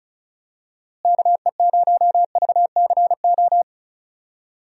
Morse code, heard as a keyed tone, about 36 wpm, 700 Hz. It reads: KE0VCO
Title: USA Call Signs - 6 Characters - Repeated 2x - 35wpm